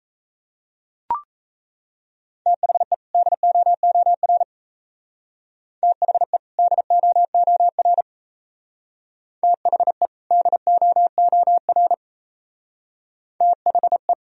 Morse code, heard as a keyed tone, about 37 wpm, 700 Hz. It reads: THE DOOR THE DOOR THE DOOR THE